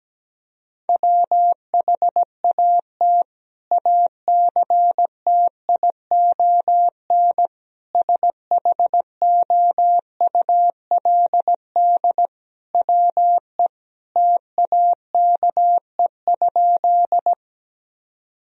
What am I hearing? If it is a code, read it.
Morse code, heard as a keyed tone, about 17 wpm, 700 Hz. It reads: WHAT ACTION SHOULD WE TAKE?
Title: Sentences from Top 1000 Words - Review Part 4 - 17wpm